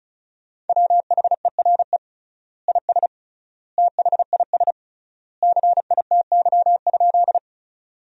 Morse code, heard as a keyed tone, about 34 wpm, 700 Hz. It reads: WHERE IS THIS CITY?
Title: 3x Word-Spacing - Sentences from Top 300 Words - 35wpm